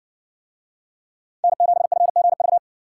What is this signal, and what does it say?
Morse code, heard as a keyed tone, about 60 wpm, 700 Hz. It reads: N7FGV